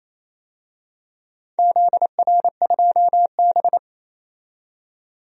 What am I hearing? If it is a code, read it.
Morse code, heard as a keyed tone, about 27 wpm, 700 Hz. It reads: ZR2B
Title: DXCC-100 Call Signs - 4 Characters - 28wpm